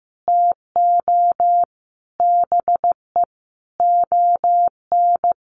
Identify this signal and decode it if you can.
Morse code, heard as a keyed tone, about 15 wpm, 700 Hz. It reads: TO BE ON
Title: Sets of 5 Words Encore2 15wpm